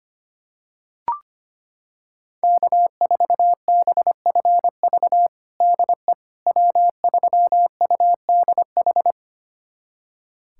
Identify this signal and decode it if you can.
Morse code, heard as a keyed tone, about 25 wpm, 700 Hz. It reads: K4BFV DE W3UDH
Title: USA Callsign DE Callsign - 5 Characters - 25wpm